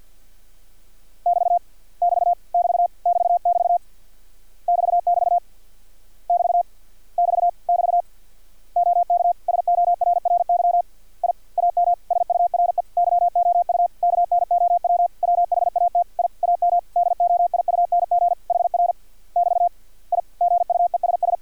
Morse code, heard as a keyed tone, about 50 wpm, 700 Hz. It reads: <BT> <BT> <BT> <BT><BT> <BT><BT> <BT> <BT> <BT> OK SORRY I AM SURE YOU KNOW WHAT I AM DOIUNG HR <BT> I GUESS
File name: archived / CRSnet